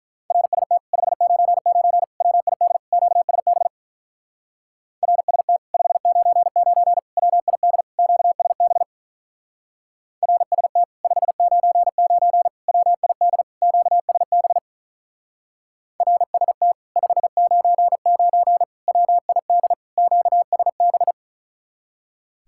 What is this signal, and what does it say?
Morse code, heard as a keyed tone, about 44 wpm, 700 Hz. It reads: RST 599 WID QSB RST 599 WID QSB RST 599 WID QSB RST 599 WID QSB